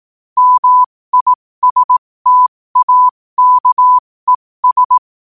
Morse code, heard as a keyed tone, about 18 wpm, spaced as for 15 wpm, 1000 Hz. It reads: MISTAKES
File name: qotdfiles / 15wpm